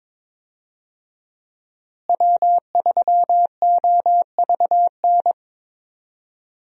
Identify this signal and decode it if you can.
Morse code, heard as a keyed tone, about 22 wpm, 700 Hz. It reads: W3OVN
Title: USA Call Signs - 5 Characters - Repeated 3x - 22wpm